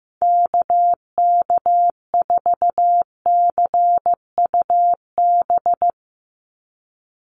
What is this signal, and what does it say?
Morse code, heard as a keyed tone, about 15 wpm, 700 Hz. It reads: KK4CUB
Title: USA Call Signs 15wpm